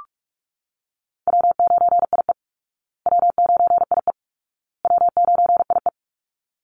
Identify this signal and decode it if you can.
Morse code, heard as a keyed tone, about 44 wpm, 700 Hz. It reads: W9IE W9IE W9IE
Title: Call Signs Repeated 2x - 45wpm